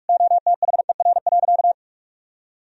Morse code, heard as a keyed tone, about 46 wpm, 700 Hz. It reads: OTHER.
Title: Sentences from Top 1000 Words - Review Part 2 - 45wpm